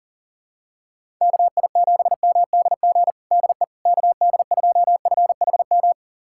Morse code, heard as a keyed tone, about 40 wpm, 700 Hz. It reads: KI7MDG DE KD2FHM
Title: USA Callsign DE Callsign 40wpm